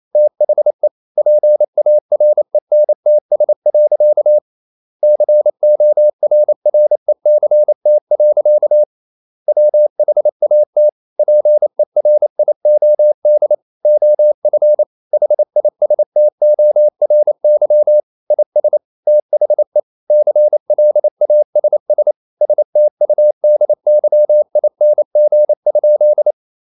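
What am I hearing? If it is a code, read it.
Morse code, heard as a keyed tone, about 28 wpm, 600 Hz. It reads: THE PARENTS. CORRECT. WHAT PERIOD OF HISTORY IS THE CLASS STUDYING?
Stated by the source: Morse code practice at 28 wpm, text of Ken Follett's novel Never in English, from CW med Gnister.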